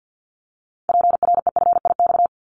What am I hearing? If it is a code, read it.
Morse code, heard as a keyed tone, about 50 wpm, 700 Hz. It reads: PREFIX
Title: Callsign Prefixes for Top 100 DXCC - 50wpm